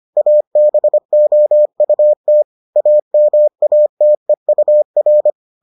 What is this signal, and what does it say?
Morse code, heard as a keyed tone, about 25 wpm, 600 Hz. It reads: ABOUT AMATEUR